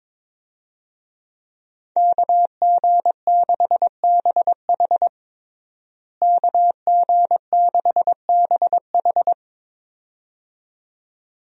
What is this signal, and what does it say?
Morse code, heard as a keyed tone, about 22 wpm, 700 Hz. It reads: KG6BH KG6BH
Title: USA Call Signs - 5 Characters - Repeated 2x - 22wpm